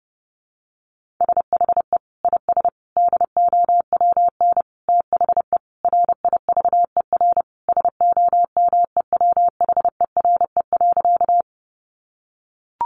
Morse code, heard as a keyed tone, about 30 wpm, 700 Hz. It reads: SHE IS DOWN THE RIVER SOMEWHERE.
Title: Sentences from Top 1400 Words 30wpm